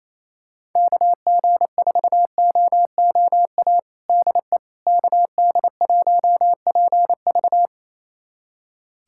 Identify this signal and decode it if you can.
Morse code, heard as a keyed tone, about 28 wpm, 700 Hz. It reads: KG4OOA DE KD1PV
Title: USA Callsign DE Callsign 28wpm